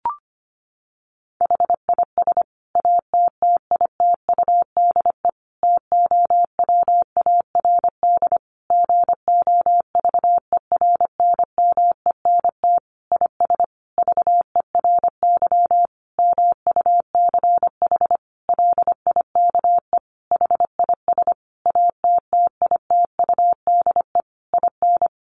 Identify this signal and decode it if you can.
Morse code, heard as a keyed tone, about 25 wpm, 700 Hz. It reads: HIS ATTITUDE TOWARD GOVERNMENT IS VERY MUCH LIKE HIS ATTITUDE IN
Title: Sentences from Top 1300 Words 25wpm